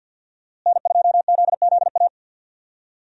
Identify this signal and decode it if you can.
Morse code, heard as a keyed tone, about 46 wpm, 700 Hz. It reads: N2ZZA
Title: USA Call Signs - 5 Characters - Repeated 2x - 50wpm